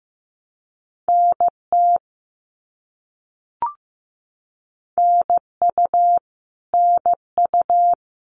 Morse code, heard as a keyed tone, about 15 wpm, 700 Hz. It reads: NT NU NU
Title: Canadian Provinces and Territories - Repeated 3x - 15wpm